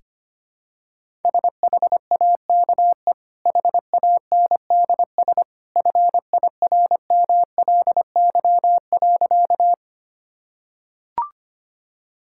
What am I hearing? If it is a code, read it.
Morse code, heard as a keyed tone, about 25 wpm, 700 Hz. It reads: SHAKE HANDS FIRMLY.
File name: Easy Sentences with Hints - 25wpm